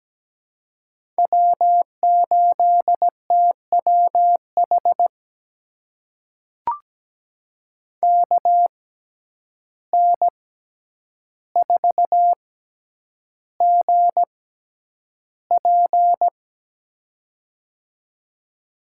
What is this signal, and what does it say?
Morse code, heard as a keyed tone, about 17 wpm, spaced as for 9 wpm, 700 Hz. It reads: W8TWH KN4GP
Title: USA Call Signs - 5 Characters - 6x Character Spacing - 17wpm